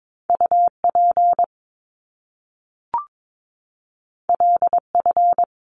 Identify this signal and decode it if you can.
Morse code, heard as a keyed tone, about 22 wpm, 700 Hz. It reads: UP LF
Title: Top Two Letter Combinations in Words 22wpm